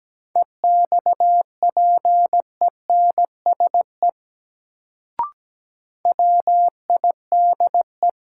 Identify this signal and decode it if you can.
Morse code, heard as a keyed tone, about 17 wpm, 700 Hz. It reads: EXPENSE WIDE
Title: Sets of 4 Words - New Word Drill - 17wpm